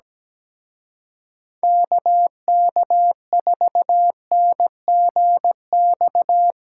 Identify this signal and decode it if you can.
Morse code, heard as a keyed tone, about 18 wpm, 700 Hz. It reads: KK4NGX
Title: Speed Racing - USA 2x3 Call Signs 17wpm